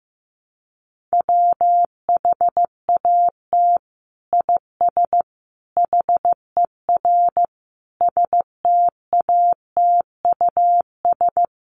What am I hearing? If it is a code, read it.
Morse code, heard as a keyed tone, about 15 wpm, 700 Hz. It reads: WHAT IS HER STATUS
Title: Sentences from Top 1400 Words 15wpm